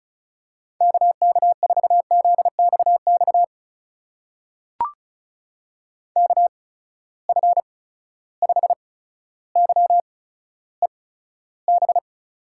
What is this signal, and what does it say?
Morse code, heard as a keyed tone, about 34 wpm, spaced as for 15 wpm, 700 Hz. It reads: KK4ZXX KF5YEB
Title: USA Call Signs - 6 Characters - 8x Character Spacing - 35wpm